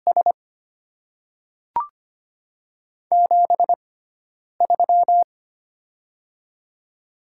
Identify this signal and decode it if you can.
Morse code, heard as a keyed tone, about 25 wpm, spaced as for 9 wpm, 700 Hz. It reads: S 73
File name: Top 20 QSO Elements - 6x Character Spacing - 25wpm